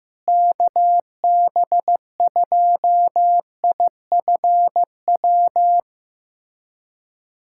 Morse code, heard as a keyed tone, about 15 wpm, 700 Hz. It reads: KB2IFW
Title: USA Callsign DE Callsign 15wpm